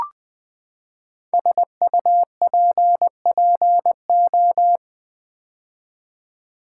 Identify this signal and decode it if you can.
Morse code, heard as a keyed tone, about 20 wpm, 700 Hz. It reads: SUPPO
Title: Build A Word - 5 Letters At A Time - 10-Letter Words - 20wpm